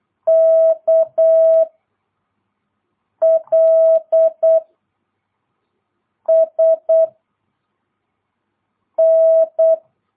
Morse code, heard as a keyed tone, about 8 wpm, 650 Hz. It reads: K L S N
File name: jh8cbh / 2022cw